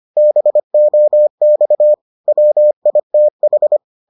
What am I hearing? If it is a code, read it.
Morse code, heard as a keyed tone, about 25 wpm, 600 Hz. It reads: BOX WITH